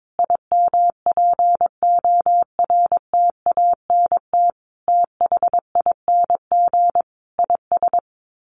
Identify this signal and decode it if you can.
Morse code, heard as a keyed tone, about 22 wpm, 700 Hz. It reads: IMPORTANT THING IS
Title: Sets of 5 Words Encore 22wpm